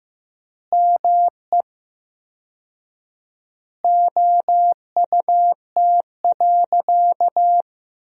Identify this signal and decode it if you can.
Morse code, heard as a keyed tone, about 16 wpm, 700 Hz. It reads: ME OUT.
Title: 4x Word-Spacing - Sentences from Top 300 Words - 15wpm